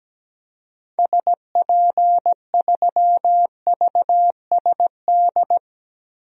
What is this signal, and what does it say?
Morse code, heard as a keyed tone, about 17 wpm, 700 Hz. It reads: SP3VSD
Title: DXCC-100 Call Signs - 6 Characters - Repeated 3x - 17wpm